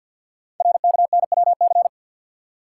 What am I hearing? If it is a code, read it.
Morse code, heard as a keyed tone, about 48 wpm, 690 Hz. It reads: RKNWC
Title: Code Groups - 5 Letters - 50wpm